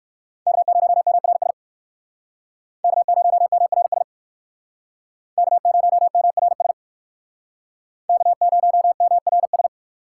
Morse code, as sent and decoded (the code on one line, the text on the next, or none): -.- ----- -- .-. ... / -.- ----- -- .-. ... / -.- ----- -- .-. ... / -.- ----- -- .-. ...
K0MRS K0MRS K0MRS K0MRS